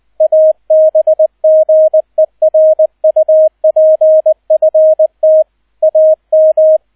Morse code, heard as a keyed tone, about 20 wpm, 630 Hz, with radio noise. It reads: ABGERUPFT AM
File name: 002 Zweiter Streich 20wpm